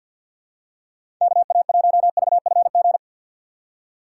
Morse code, heard as a keyed tone, about 53 wpm, 700 Hz. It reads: KA1VFG
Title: USA Call Signs - 6x Character Spacing - 50wpm